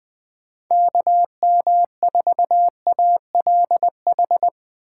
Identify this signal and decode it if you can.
Morse code, heard as a keyed tone, about 20 wpm, 700 Hz. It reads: KM4ALH
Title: USA Call Signs - 4x Character Spacing - 20wpm